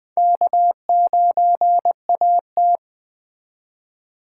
Morse code, heard as a keyed tone, about 21 wpm, 700 Hz. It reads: K9AT